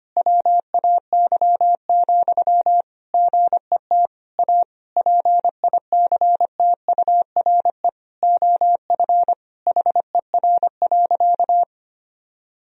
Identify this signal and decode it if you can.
Morse code, heard as a keyed tone, about 25 wpm, 700 Hz. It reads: WAY, GET A PICTURE OF HER.
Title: Sentences from Top 1000 Words - Review Part 1 - 25wpm